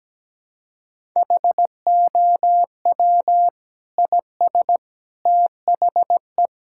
Morse code, heard as a keyed tone, about 17 wpm, 700 Hz. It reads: HOW IS THE
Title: Sentences from Top 1000 Words - Review Part 6 - 17wpm